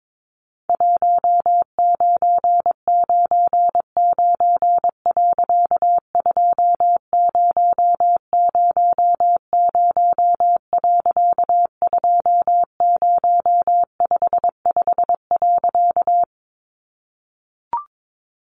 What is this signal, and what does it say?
Morse code, heard as a keyed tone, about 22 wpm, 700 Hz. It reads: 1999.2000.2055.
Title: A Sound Of Thunder 22wpm